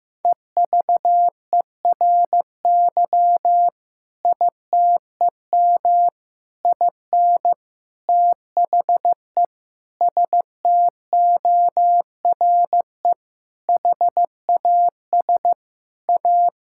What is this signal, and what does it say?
Morse code, heard as a keyed tone, about 15 wpm, 700 Hz. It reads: EVERY ITEM IN THE STORE HAS A
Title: Sentences from Top 1000 Words - Review Part 6 - 15wpm